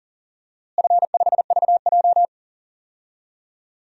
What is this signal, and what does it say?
Morse code, heard as a keyed tone, about 39 wpm, 690 Hz. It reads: F5VJ
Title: DXCC-100 Call Signs - 4 Characters - Repeated 2x - 40wpm